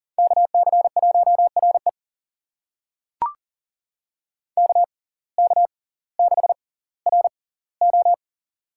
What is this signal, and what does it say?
Morse code, heard as a keyed tone, about 40 wpm, spaced as for 20 wpm, 700 Hz. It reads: KC1RE KK6RO
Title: USA Call Signs - 6x Character Spacing - 40wpm